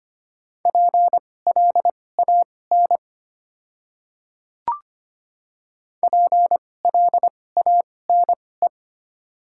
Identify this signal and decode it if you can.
Morse code, heard as a keyed tone, about 25 wpm, spaced as for 18 wpm, 700 Hz. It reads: PLAN PLANE
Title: 2x Character Spacing - Top 500 Words - 25wpm